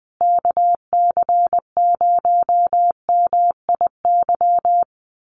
Morse code, heard as a keyed tone, about 20 wpm, 700 Hz. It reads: KC0MIY
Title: USA Call Signs Repeated 3x 20wpm